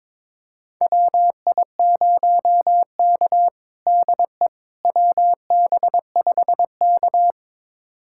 Morse code, heard as a keyed tone, about 22 wpm, 700 Hz. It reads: WI0K DE WB5K
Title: USA Callsign DE Callsign - 4 Characters - 22wpm